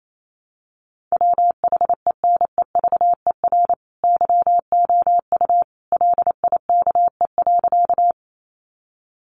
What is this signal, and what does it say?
Morse code, heard as a keyed tone, about 28 wpm, 700 Hz. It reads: WHENEVER YOU LIKE.
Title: Sentences from Top 1200 Words 28wpm